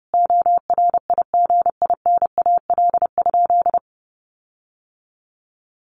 Morse code, heard as a keyed tone, about 30 wpm, 700 Hz. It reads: ORIGINAL?
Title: Sentences from Top 1500 Words 30wpm